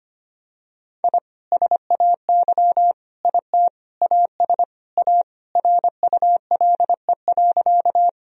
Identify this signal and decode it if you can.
Morse code, heard as a keyed tone, about 25 wpm, 700 Hz. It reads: I SAY IT AS A RULE.
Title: Sentences from Top 1000 Words - Review Part 2 - 25wpm